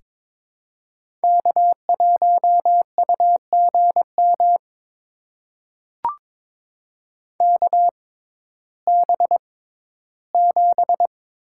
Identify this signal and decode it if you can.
Morse code, heard as a keyed tone, about 22 wpm, spaced as for 14 wpm, 700 Hz. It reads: K1UGM KB7
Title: USA Call Signs - 6x Character Spacing - 22wpm